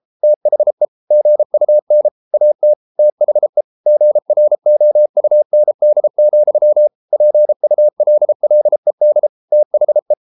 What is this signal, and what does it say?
Morse code, heard as a keyed tone, about 33 wpm, 600 Hz. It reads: THE GUN AT THE GROUND, PULLED THE